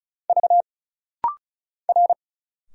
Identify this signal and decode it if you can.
Morse code, heard as a keyed tone, about 36 wpm, 700 Hz. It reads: V R
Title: Single Letters - Rapid-Fire 35wpm